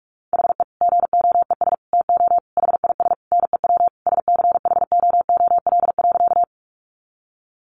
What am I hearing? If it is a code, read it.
Morse code, heard as a keyed tone, about 44 wpm, 700 Hz. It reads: HE GOES TO HIS NEW SCHOOL.
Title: Easy Phrases 45wpm